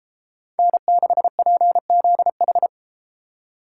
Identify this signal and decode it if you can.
Morse code, heard as a keyed tone, about 33 wpm, 700 Hz. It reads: N6PZH